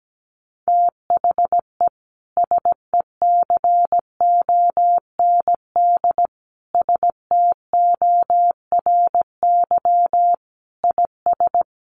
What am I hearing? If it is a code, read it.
Morse code, heard as a keyed tone, about 17 wpm, 700 Hz. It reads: THE SECOND STORY IS
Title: Sentences from Top 1400 Words 17wpm